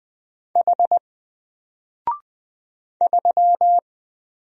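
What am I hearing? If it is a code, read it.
Morse code, heard as a keyed tone, about 20 wpm, 700 Hz. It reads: H 3
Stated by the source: Single letter and number Morse drill at 20 wpm, from N4LCD.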